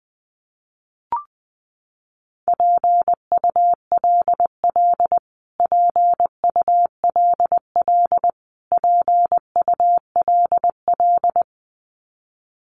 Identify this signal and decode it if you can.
Morse code, heard as a keyed tone, about 20 wpm, 700 Hz. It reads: PULL PULL PULL
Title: Top 500 Words Repeated 20wpm